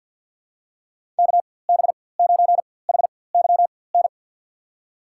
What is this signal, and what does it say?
Morse code, heard as a keyed tone, about 53 wpm, spaced as for 24 wpm, 700 Hz. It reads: KB9HYN
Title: USA Call Signs - 4x Character Spacing - 50wpm